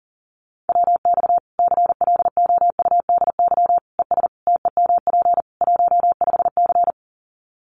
Morse code, heard as a keyed tone, about 40 wpm, 700 Hz. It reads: WX CLOUDY ES TEMP 15C
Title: Common QSO Phrases 40wpm